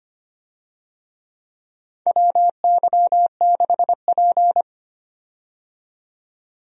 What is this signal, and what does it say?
Morse code, heard as a keyed tone, about 25 wpm, 700 Hz. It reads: WY6P
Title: USA Call Signs - 4 Characters - 25wpm